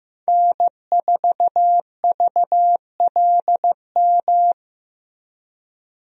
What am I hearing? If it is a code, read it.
Morse code, heard as a keyed tone, about 15 wpm, 700 Hz. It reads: N4VLM
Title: USA Call Signs - 5 Characters - Repeated 2x - 15wpm